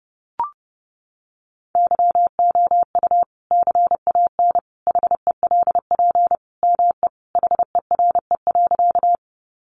Morse code, heard as a keyed tone, about 30 wpm, 700 Hz. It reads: YOU CAN HELP ME HERE.
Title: Sentences from Top 200 Words 30wpm